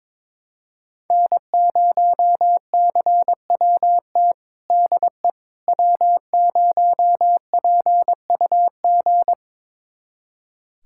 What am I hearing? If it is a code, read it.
Morse code, heard as a keyed tone, about 22 wpm, 700 Hz. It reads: N0CWT DE W0PUG
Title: USA Callsign DE Callsign - 5 Characters - 22wpm